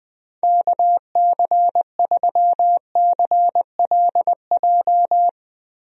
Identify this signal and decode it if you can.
Morse code, heard as a keyed tone, about 21 wpm, 700 Hz. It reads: KC3CLJ
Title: USA Call Signs - 6 Characters - Repeated 3x - 20wpm